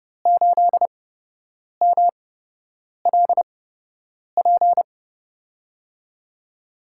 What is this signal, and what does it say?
Morse code, heard as a keyed tone, about 30 wpm, spaced as for 8 wpm, 700 Hz. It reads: 8MLP
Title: USA Call Signs - 5 Characters - 8x Character Spacing - 30wpm